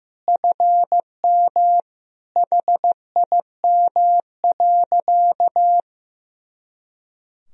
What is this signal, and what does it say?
Morse code, heard as a keyed tone, about 15 wpm, 700 Hz. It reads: FM HIM.
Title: Sentences from Top 400 Words and Abbreviations 15wpm